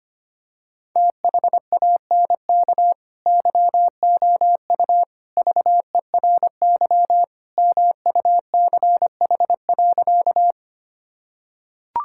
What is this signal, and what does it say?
Morse code, heard as a keyed tone, about 25 wpm, 700 Hz. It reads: THANK YOU VERY MUCH.
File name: practice / 25wpm / Sentences with 3-Word Phrases - 25wpm